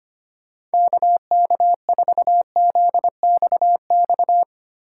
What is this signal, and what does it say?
Morse code, heard as a keyed tone, about 25 wpm, 700 Hz. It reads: KK4ZXX
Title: USA Call Signs - 4x Character Spacing - 25wpm